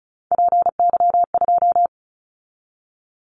Morse code, heard as a keyed tone, about 36 wpm, 700 Hz. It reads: PY2
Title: Callsign Prefixes for Top 50 DXCC - 35wpm